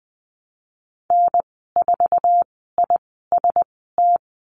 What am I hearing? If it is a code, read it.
Morse code, heard as a keyed tone, about 20 wpm, spaced as for 14 wpm, 700 Hz. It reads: N4IST
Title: 062 Lesson 11g - Letter h - Call Signs 20wpm